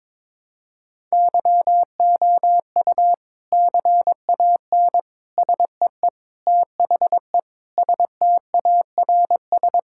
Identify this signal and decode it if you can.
Morse code, heard as a keyed tone, about 22 wpm, 700 Hz. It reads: YOU CAN SEE THE STARS
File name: practice / 22wpm / Sentences with 3-Word Phrases - 22wpm